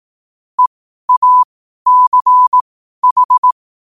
Morse code, heard as a keyed tone, about 18 wpm, spaced as for 12 wpm, 1000 Hz. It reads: EACH
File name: qotdfiles / 12wpm